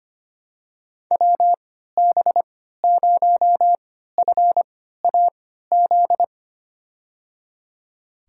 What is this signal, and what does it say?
Morse code, heard as a keyed tone, about 25 wpm, spaced as for 14 wpm, 700 Hz. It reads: WB0FAZ
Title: USA Call Signs - 6 Characters - 3x Character Spacing - 25wpm